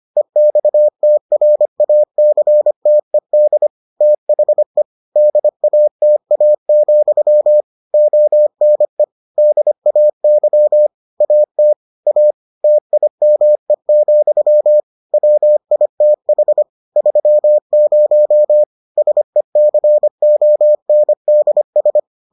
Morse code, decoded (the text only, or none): EXTRACTED THE DATA, ONE DAY AT A TIME, WITH 30 SECONDS